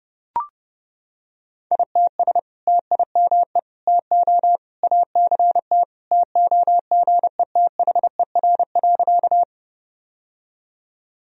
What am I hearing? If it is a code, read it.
Morse code, heard as a keyed tone, about 32 wpm, 700 Hz. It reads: ITS TIME TO ACT TOGETHER.
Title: Sentences from Top 1000 Words - Review Part 1 - 30wpm